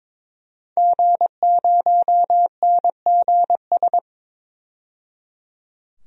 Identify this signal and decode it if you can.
Morse code, heard as a keyed tone, about 21 wpm, 700 Hz. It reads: G0NGS
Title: DXCC-100 Call Signs - 5 Characters - 22wpm